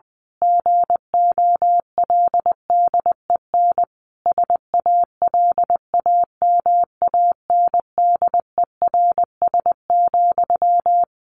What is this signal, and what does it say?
Morse code, heard as a keyed tone, about 20 wpm, 700 Hz. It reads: GOLDEN SALAMANDERS,
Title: A Sound Of Thunder 20wpm